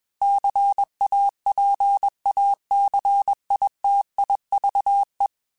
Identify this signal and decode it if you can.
Morse code, heard as a keyed tone, about 21 wpm, 800 Hz. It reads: CAPACITIVE
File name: Technical Subjects / Technical Subjects-mp3s / XC @20